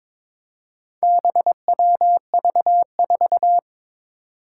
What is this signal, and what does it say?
Morse code, heard as a keyed tone, about 21 wpm, 700 Hz. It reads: BWV4
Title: Four Challenging Letter-Number Combinations 22wpm